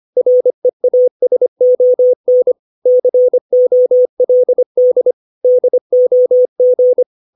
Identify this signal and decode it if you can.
Morse code, heard as a keyed tone, about 25 wpm, 500 Hz. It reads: REASON COLD DOG
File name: 632 words / 632 Words 25WPM